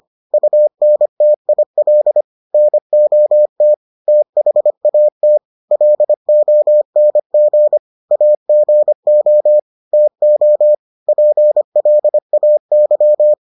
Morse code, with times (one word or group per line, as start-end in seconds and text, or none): 0.3-2.2 UNTIL
2.5-3.7 NOT
4.1-5.4 THAT
5.7-7.8 LONG
8.1-9.6 AGO
9.9-10.7 TO
11.1-13.3 PLAY